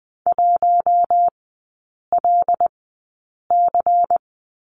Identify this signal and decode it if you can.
Morse code, heard as a keyed tone, about 20 wpm, spaced as for 8 wpm, 700 Hz. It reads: 1LC
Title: 093 Lesson 16e - Letter c - Three Characters 20wpm